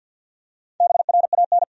Morse code, heard as a keyed tone, about 52 wpm, 700 Hz. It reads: BRAN
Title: USA Common Names - Max 4 Letters - 50wpm